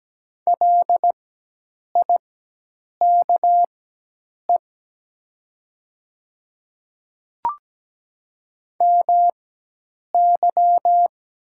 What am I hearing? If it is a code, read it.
Morse code, heard as a keyed tone, about 17 wpm, spaced as for 8 wpm, 700 Hz. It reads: LIKE MY
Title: 4x Character Spacing - Top 100 Words - 17wpm